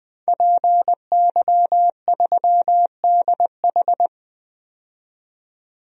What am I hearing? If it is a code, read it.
Morse code, heard as a keyed tone, about 20 wpm, 700 Hz. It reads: PY3DH
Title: DXCC-100 Call Signs - 5 Characters - 20wpm